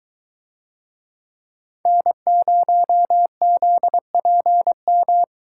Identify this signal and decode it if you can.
Morse code, heard as a keyed tone, about 23 wpm, 700 Hz. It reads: N0ZPM